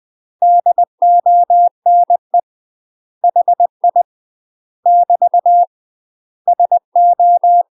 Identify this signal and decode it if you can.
Morse code, heard as a keyed tone, about 20 wpm, 700 Hz. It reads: DONE HI <BT> SO
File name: UK MASTER QSOs TRAINING/20WPM